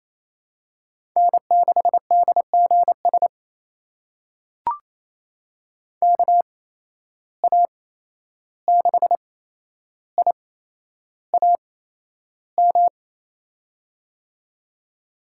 Morse code, heard as a keyed tone, about 28 wpm, spaced as for 11 wpm, 700 Hz. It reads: N6DGS KA6IAM
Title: USA Call Signs - 8x Character Spacing - 28wpm